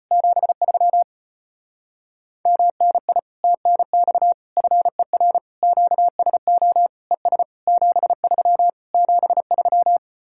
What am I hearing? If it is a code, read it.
Morse code, heard as a keyed tone, about 33 wpm, 700 Hz. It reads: 73 MNI TNX FER QSO ES 73 73